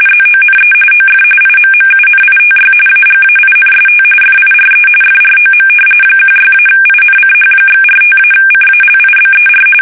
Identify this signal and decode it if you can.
RTTY, about 45.45 baud. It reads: ZONE OF IMMUNITY AGAINST FIRE FROM 16-INCH/45-CALIBER
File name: fsk850 / battleshipnj